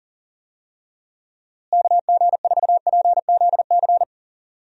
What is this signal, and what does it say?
Morse code, heard as a keyed tone, about 40 wpm, 700 Hz. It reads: KG4PZC
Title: USA Call Signs - 6 Characters - Repeated 3x - 40wpm